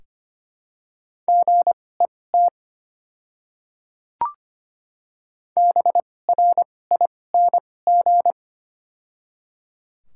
Morse code, heard as a keyed tone, about 25 wpm, spaced as for 18 wpm, 690 Hz. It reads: GET BRING
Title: 2x Character Spacing - Top 500 Words - 25wpm